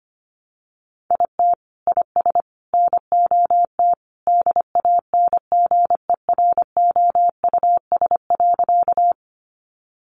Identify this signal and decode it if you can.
Morse code, heard as a keyed tone, about 25 wpm, 700 Hz. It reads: IT IS NOT DANGEROUS.
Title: Sentences from Top 1200 Words 25wpm